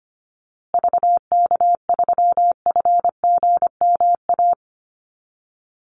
Morse code, heard as a keyed tone, about 25 wpm, 700 Hz. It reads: VK3FGMA
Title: USA Call Signs Repeated 2x 25wpm